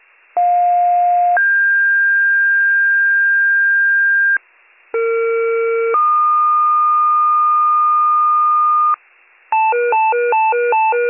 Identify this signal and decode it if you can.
Two-tone page, heard as a one-second tone, then a three-second tone, about 710 then 1600 Hz, about 460 then 1220 Hz, with noise. It is heard from 0.4 to 8.9 s.